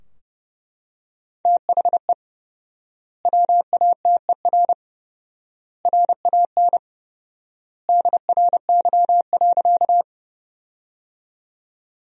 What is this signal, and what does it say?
Morse code, heard as a keyed tone, about 32 wpm, 700 Hz. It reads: THE WATER RAN DRY.
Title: 4x Word-Spacing - Sentences from Top 500 Words - 30wpm